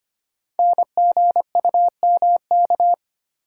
Morse code, heard as a keyed tone, about 25 wpm, 700 Hz. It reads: NGUMK
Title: Code Groups - 5 Letters - 25wpm